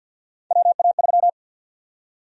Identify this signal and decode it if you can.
Morse code, heard as a keyed tone, about 52 wpm, 700 Hz. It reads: WA3
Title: USA Call Sign Prefixes - 50wpm